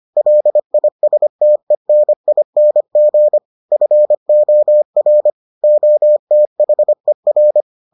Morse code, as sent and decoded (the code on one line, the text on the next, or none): .-.. .. ... - . -. .. -. --. / ..-. --- .-. / --- - .... .-.
LISTENING FOR OTHER